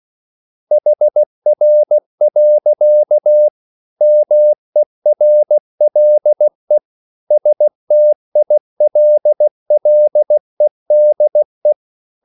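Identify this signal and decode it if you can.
Morse code, heard as a keyed tone, about 16 wpm, 600 Hz. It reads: HR. MERLE STILLEDE
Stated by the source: Slow (QRS) Morse code at 16 wpm, from CW med Gnister.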